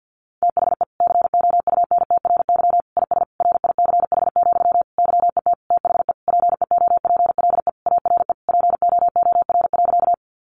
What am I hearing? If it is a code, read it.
Morse code, heard as a keyed tone, about 50 wpm, 700 Hz. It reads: THE COUNTRY IS RICH, YET THE PEOPLE ARE POOR.
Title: Sentences from Top 1100 Words 50wpm